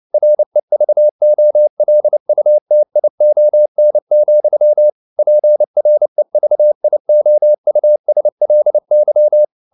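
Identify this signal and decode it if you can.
Morse code, heard as a keyed tone, about 29 wpm, 600 Hz. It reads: REVOLUTION, PREVIOUSLY
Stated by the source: Morse code at 29 wpm, sending Ken Follett's novel Never in English, from CW med Gnister.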